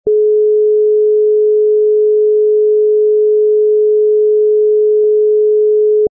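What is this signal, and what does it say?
Dial tone (Europe, 425 Hz).
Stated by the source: Dial tone (lite) from the Motorola M1000 network tone set.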